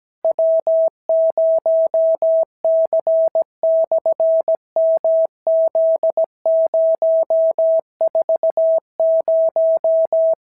Morse code, heard as a keyed tone, about 17 wpm, 650 Hz. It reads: W0C/MZ040
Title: SOTA QSO Practice 17wpm